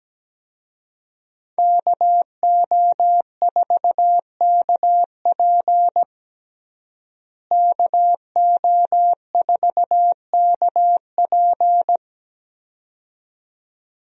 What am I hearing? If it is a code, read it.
Morse code, heard as a keyed tone, about 17 wpm, 700 Hz. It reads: KO4KP KO4KP
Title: USA Call Signs - 5 Characters - Repeated 2x - 17wpm